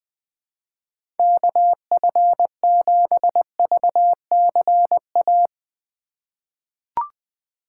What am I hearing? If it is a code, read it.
Morse code, heard as a keyed tone, about 21 wpm, 700 Hz. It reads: KF7VCA
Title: USA Call Signs - 6 Characters - 3x Character Spacing - 20wpm